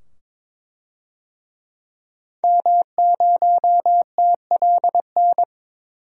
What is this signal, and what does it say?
Morse code, heard as a keyed tone, about 22 wpm, 700 Hz. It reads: M0TLN
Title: DXCC-100 Call Signs - 5 Characters - 22wpm